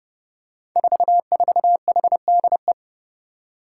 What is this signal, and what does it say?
Morse code, heard as a keyed tone, about 30 wpm, 700 Hz. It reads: 44HDE
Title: Five Challenging Letter-Number Combinations 30wpm